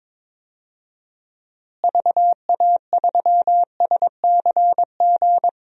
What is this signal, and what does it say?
Morse code, heard as a keyed tone, about 21 wpm, 700 Hz. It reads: VA3SCG
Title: DXCC-100 Call Signs - 6 Characters - Repeated 2x - 22wpm